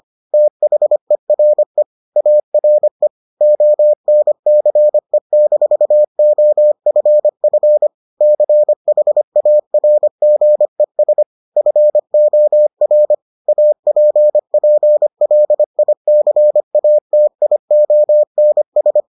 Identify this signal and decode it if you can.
Morse code, heard as a keyed tone, about 25 wpm, 600 Hz. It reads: THERE ARE ONCE-OFF CHARGES FOR APPLICATIONS